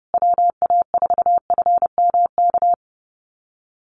Morse code, heard as a keyed tone, about 30 wpm, 700 Hz. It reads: WA4FMK
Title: USA Call Signs Repeated 3x 30wpm